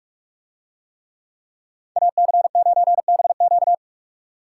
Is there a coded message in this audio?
Yes: Morse code from 2.0 to 3.7 s, heard as a keyed tone, about 45 wpm, 700 Hz. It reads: AC9BQ